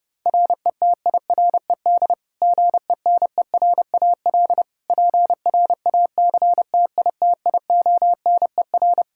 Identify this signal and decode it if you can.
Morse code, heard as a keyed tone, about 30 wpm, 700 Hz. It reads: RETIRED GENERAL PRACTITIONER
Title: Practice QSO - Exchange 3 - More Info - 30wpm